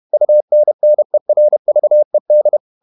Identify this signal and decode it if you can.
Morse code, heard as a keyed tone, about 31 wpm, 600 Hz. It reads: UNNERVED
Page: Never 31wpm | CW med Gnister